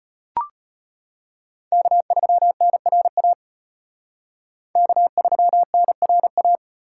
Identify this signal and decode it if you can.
Morse code, heard as a keyed tone, about 32 wpm, 700 Hz. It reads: K3NRA K3NRA